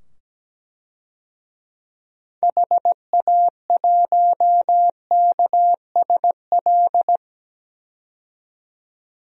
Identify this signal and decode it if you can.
Morse code, heard as a keyed tone, about 17 wpm, 700 Hz. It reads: HA1KSL
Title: DXCC-100 Call Signs - 6 Characters - 17wpm